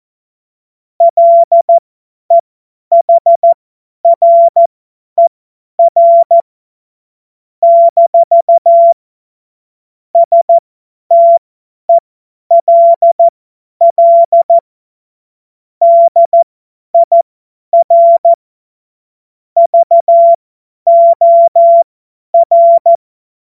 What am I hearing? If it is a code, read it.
Morse code, heard as a keyed tone, about 14 wpm, spaced as for 10 wpm, 680 Hz. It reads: LEHRER - STELL DIR VOR
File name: Morsekurs / Witze / Witz 03. August 2021, 50 F70